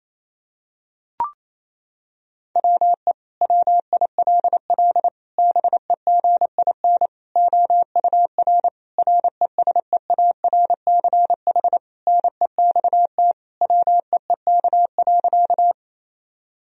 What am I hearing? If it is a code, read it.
Morse code, heard as a keyed tone, about 28 wpm, 700 Hz. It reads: WE WILL BEGIN OUR RESEARCH NEXT WEEK.
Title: Sentences from Top 1000 Words - Review Part 4 - 28wpm